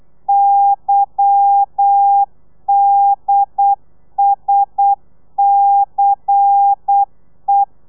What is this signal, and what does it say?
Morse code, heard as a keyed tone, about 8 wpm, 800 Hz. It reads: YDSCE
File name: telegrafia / i2tfj / Lez10 35